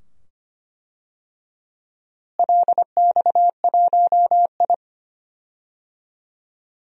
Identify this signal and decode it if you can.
Morse code, heard as a keyed tone, about 25 wpm, 700 Hz. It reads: LX1I
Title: DXCC-100 Call Signs - 4 Characters - Repeated 2x - 25wpm